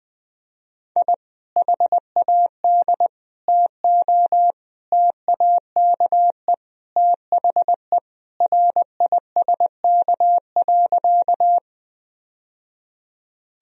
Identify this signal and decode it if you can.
Morse code, heard as a keyed tone, about 20 wpm, 700 Hz. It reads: I HAD TO TAKE THE RISK.
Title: Sentences from Top 1000 Words - Review Part 4 - 20wpm